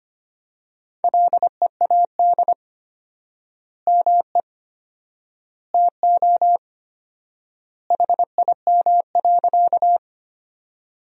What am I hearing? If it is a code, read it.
Morse code, heard as a keyed tone, about 25 wpm, 700 Hz. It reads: LEAD ME TO HIM.
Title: 4x Word-Spacing - Sentences from Top 500 Words - 25wpm